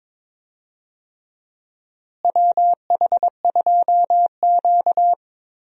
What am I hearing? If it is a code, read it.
Morse code, heard as a keyed tone, about 22 wpm, 700 Hz. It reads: WH2Q
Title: USA Call Signs - 4 Characters - Repeated 3x - 22wpm